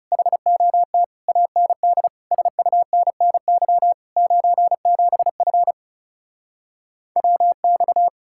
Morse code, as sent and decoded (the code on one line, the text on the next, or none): .... --- - / .- -. -.. / ... ..- -. -. -.-- / ----. --... ..-. / .-- -..-
HOT AND SUNNY 97F WX